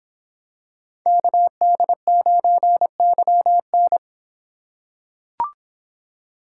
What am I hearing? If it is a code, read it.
Morse code, heard as a keyed tone, about 26 wpm, 700 Hz. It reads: KD9YN